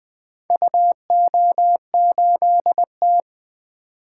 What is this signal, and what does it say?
Morse code, heard as a keyed tone, about 20 wpm, 700 Hz. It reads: UO8T
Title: Four Letter-Number Combinations - 20wpm